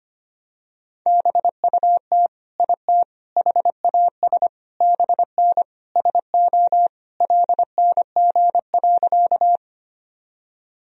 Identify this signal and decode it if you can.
Morse code, heard as a keyed tone, about 25 wpm, 700 Hz. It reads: BUT IT HAS BN SO LNG.
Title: Sentences from Top 100 Words and Abbreviations 25wpm